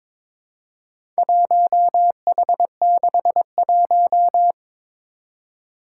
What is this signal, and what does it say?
Morse code, heard as a keyed tone, about 22 wpm, 700 Hz. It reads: 1H61
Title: Four Challenging Letter-Number Combinations 22wpm